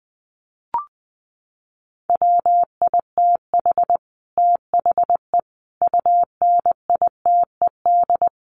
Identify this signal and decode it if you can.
Morse code, heard as a keyed tone, about 20 wpm, 700 Hz. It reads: WITH THE UNITED
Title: Sets of 4 Words 20wpm